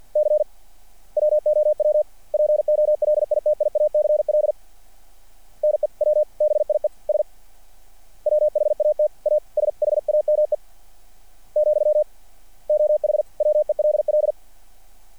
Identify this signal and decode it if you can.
Morse code, heard as a keyed tone, about 49 wpm, 600 Hz. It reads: C WOW POLITIACL NEWBIES WHAT A SHAME , OH WELL
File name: archived / CRSnet